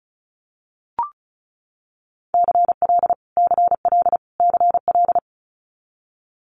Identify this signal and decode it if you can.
Morse code, heard as a keyed tone, about 36 wpm, 700 Hz. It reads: CL CL CL
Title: Multi-letter Phonemes Repeated 3x 35wpm